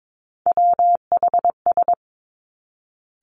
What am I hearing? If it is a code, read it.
Morse code, heard as a keyed tone, about 22 wpm, 700 Hz. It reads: WHS
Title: Three Challenging Letter-Number Combinations 22wpm